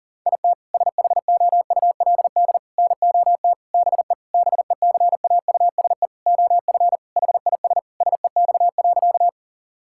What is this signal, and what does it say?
Morse code, heard as a keyed tone, about 40 wpm, 700 Hz. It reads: IT SHOULD NOT BE BECAUSE OF HIS SEX.
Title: Sentences from Top 1600 Words 40wpm